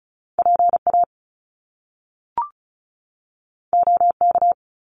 Morse code, heard as a keyed tone, about 36 wpm, 700 Hz. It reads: PA OK
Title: Top Two Letter Combinations in Words 35wpm